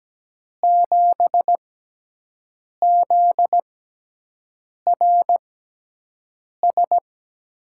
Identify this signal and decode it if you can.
Morse code, heard as a keyed tone, about 17 wpm, spaced as for 6 wpm, 700 Hz. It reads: 7ZRS
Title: USA Call Signs - 6 Characters - 6x Character Spacing - 17wpm